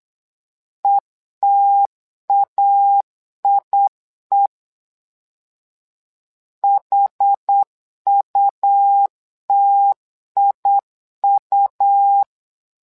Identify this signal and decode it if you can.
Morse code, heard as a keyed tone, about 8 wpm, 800 Hz. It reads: ETAIE HUTIU
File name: LEZIONI / 2 UHDN / 2 UH 8wpm 2129